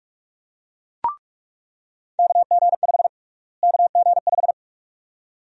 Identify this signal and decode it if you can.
Morse code, heard as a keyed tone, about 47 wpm, 700 Hz. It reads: KG5 KG5
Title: USA Call Sign Prefixes - Repeated 2x - 45wpm